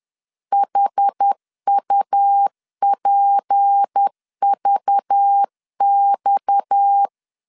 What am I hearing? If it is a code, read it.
Morse code, heard as a keyed tone, about 10 wpm, 800 Hz. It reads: HUPVX
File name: LEZIONI / 7 GF9Z / 7 GF9Z 10wpm 2611